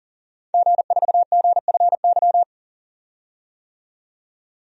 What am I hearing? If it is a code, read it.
Morse code, heard as a keyed tone, about 40 wpm, 700 Hz. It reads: G4GFY
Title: DXCC-100 Call Signs - 5 Characters - 40wpm